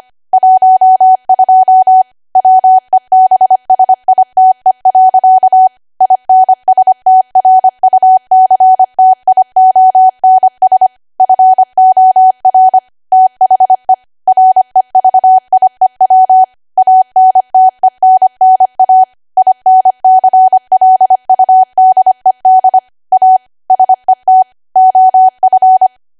Morse code, heard as a keyed tone, about 25 wpm, 750 Hz. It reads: 12 WEBSITE. INSTRUCTIONS FOR THE REVIEW ANTENNA INCLUDED A SET OF
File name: Archive / 25 WPM / 25WPM